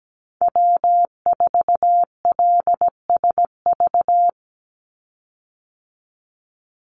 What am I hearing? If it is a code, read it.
Morse code, heard as a keyed tone, about 17 wpm, 700 Hz. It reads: W4LSV
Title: Call Signs Repeated 2x - 17wpm